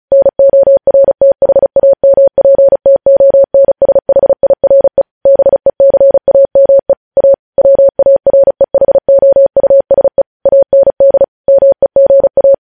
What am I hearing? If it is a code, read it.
Morse code, heard as a keyed tone, about 36 wpm, 550 Hz. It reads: NORTHAMPTONSHIRE BECAME A WAREHOUSE AND MEGA
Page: news headlines in morse code at 35 wpm